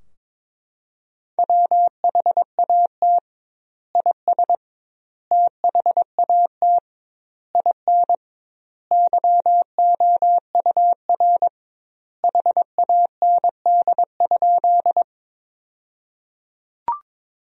Morse code, heard as a keyed tone, about 23 wpm, 700 Hz. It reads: WHAT IS THAT IN YOUR HAND?
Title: 2x Word-Spacing - Sentences from Top 200 Words - 22wpm